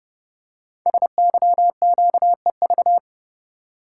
Morse code, heard as a keyed tone, about 30 wpm, 690 Hz. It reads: SYQEV
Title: Code Groups - 5 Letters - 30wpm